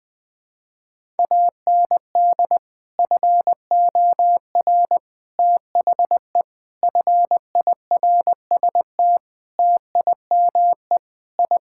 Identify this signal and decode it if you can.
Morse code, heard as a keyed tone, about 21 wpm, 700 Hz. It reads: AND FOR THE FIRST TIME I
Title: Sentences with 5-Word Phrases - 20wpm